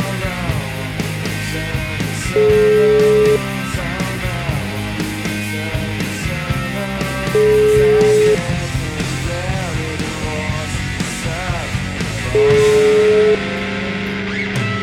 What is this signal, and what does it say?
Ringback tone (Europe, 425 Hz), with noise.